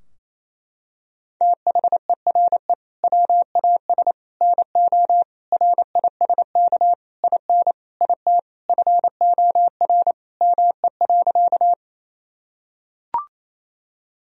Morse code, heard as a keyed tone, about 28 wpm, 700 Hz. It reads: THERE WAS NO RISK IN IT FOR ME.
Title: Sentences from Top 1000 Words - Review Part 4 - 28wpm